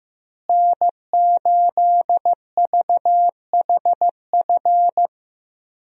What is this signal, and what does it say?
Morse code, heard as a keyed tone, about 15 wpm, 700 Hz. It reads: N8VHF
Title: USA Call Signs - 5 Characters - Repeated 2x - 15wpm